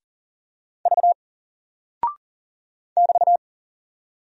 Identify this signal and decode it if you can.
Morse code, heard as a keyed tone, about 39 wpm, 700 Hz. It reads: V <BT>
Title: Single Character (Letter:Number:Punctuation:Prosign) 40wpm